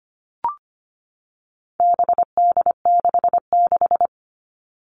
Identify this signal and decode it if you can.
Morse code, heard as a keyed tone, about 25 wpm, 700 Hz. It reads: BD66
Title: All Cut Numbers - 4-Digit 25wpm